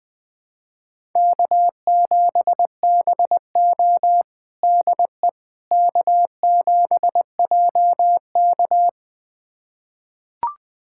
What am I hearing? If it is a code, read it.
Morse code, heard as a keyed tone, about 21 wpm, 700 Hz. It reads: K7BO DE K7JK
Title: USA Callsign DE Callsign - 4 Characters - 20wpm